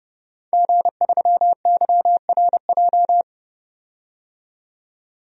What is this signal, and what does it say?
Morse code, heard as a keyed tone, about 30 wpm, 700 Hz. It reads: G3YRJ
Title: DXCC-100 Call Signs - 5 Characters - 30wpm